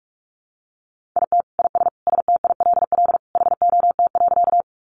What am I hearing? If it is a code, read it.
Morse code, heard as a keyed tone, about 44 wpm, 700 Hz. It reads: IT IS STILL HOT.
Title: Easy Phrases 45wpm